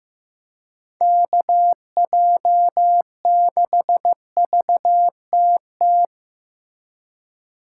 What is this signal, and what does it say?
Morse code, heard as a keyed tone, about 17 wpm, 700 Hz. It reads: KJ6VTT
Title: Speed Racing - USA 2x3 Call Signs 15wpm